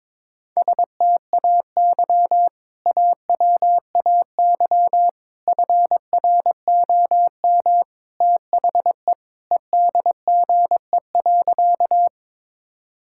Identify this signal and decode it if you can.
Morse code, heard as a keyed tone, about 22 wpm, 700 Hz. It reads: STAY AWAY FROM THE EDGE.
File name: practice / 22wpm / Sentences with 3-Word Phrases - 22wpm